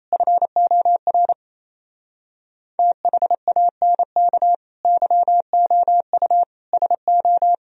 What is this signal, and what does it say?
Morse code, heard as a keyed tone, about 28 wpm, 700 Hz. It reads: FOR THANK YOU SO
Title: Speed-Racing - Sets of 5 Words 28wpm